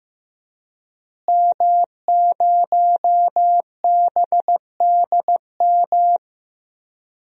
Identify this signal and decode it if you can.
Morse code, heard as a keyed tone, about 15 wpm, 700 Hz. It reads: M0BDM
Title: DXCC-100 Call Signs - 5 Characters - Repeated 3x - 15wpm